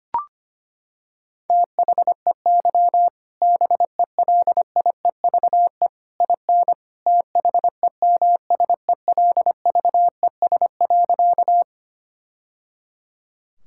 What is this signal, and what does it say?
Morse code, heard as a keyed tone, about 25 wpm, 700 Hz. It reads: THEY BELIEVE IN THEMSELVES.
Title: Sentences from Top 1000 Words - Review Part 5 - 25wpm